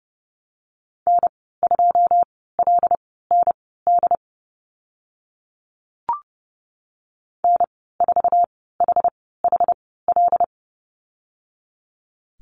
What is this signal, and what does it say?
Morse code, heard as a keyed tone, about 30 wpm, spaced as for 17 wpm, 700 Hz. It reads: N2LND N4HHL
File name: Lesson 14g - Number 2 - Call Signs 30wpm